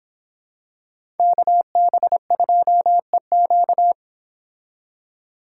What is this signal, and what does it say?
Morse code, heard as a keyed tone, about 26 wpm, 700 Hz. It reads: KB2EQ